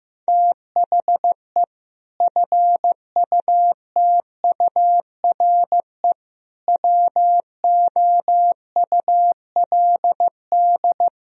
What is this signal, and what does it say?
Morse code, heard as a keyed tone, about 15 wpm, 700 Hz. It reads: THE FUTURE WOULD
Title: Sentences from Top 1000 Words - Review Part 3 - 15wpm